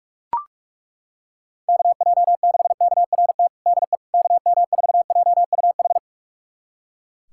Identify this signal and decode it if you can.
Morse code, heard as a keyed tone, about 41 wpm, 700 Hz. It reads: KJ6KRT DE KM4JUH
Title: USA Callsign DE Callsign 45wpm